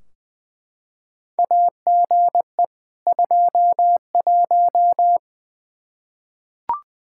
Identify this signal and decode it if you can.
Morse code, heard as a keyed tone, about 20 wpm, 700 Hz. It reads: AGE 21
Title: Practice QSO - Exchange 3 - More Info - 20wpm